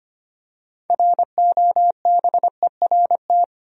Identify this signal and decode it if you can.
Morse code, heard as a keyed tone, about 25 wpm, 700 Hz. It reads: ROBERT
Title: USA Common Names - 25wpm